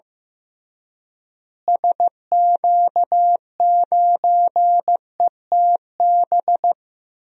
Morse code, heard as a keyed tone, about 15 wpm, 700 Hz. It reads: SQ9ETB
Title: DXCC-100 Call Signs - 6 Characters - Repeated 2x - 15wpm